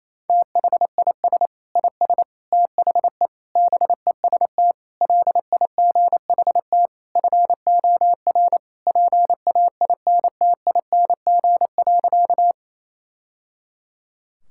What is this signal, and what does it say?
Morse code, heard as a keyed tone, about 28 wpm, 690 Hz. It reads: THIS IS THE BEST LIGHT FOR PAINTING.
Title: Sentences from Top 1000 Words - Review Part 6 - 28wpm